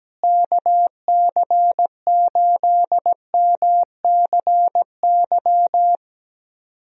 Morse code, heard as a keyed tone, about 17 wpm, 700 Hz. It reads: KC8MCY
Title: USA Call Signs - 6 Characters - Repeated 3x - 17wpm